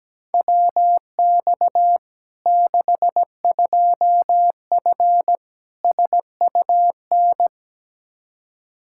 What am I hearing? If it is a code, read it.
Morse code, heard as a keyed tone, about 17 wpm, 700 Hz. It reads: WX 62F SUN
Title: Practice QSO - Exchange 2 - Info - 17wpm